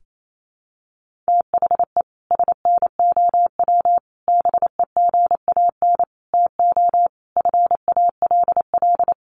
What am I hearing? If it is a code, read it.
Morse code, heard as a keyed tone, about 28 wpm, 700 Hz. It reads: THE SNOW BEGAN TO FALL
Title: Sentences from Top 1500 Words 28wpm